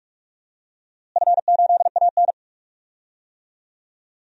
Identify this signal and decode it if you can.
Morse code, heard as a keyed tone, about 46 wpm, 700 Hz. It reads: F8AN
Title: DXCC-100 Call Signs - 4 Characters - Repeated 2x - 45wpm